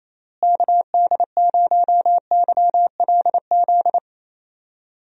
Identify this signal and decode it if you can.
Morse code, heard as a keyed tone, about 28 wpm, 700 Hz. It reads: KD0YLZ
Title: USA Call Signs - 6 Characters - Repeated 2x - 28wpm